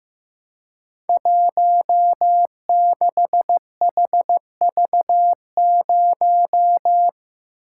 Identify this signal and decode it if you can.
Morse code, heard as a keyed tone, about 15 wpm, 700 Hz. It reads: 16HV0
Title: Code Groups - 5 Letter:Numbers - 15wpm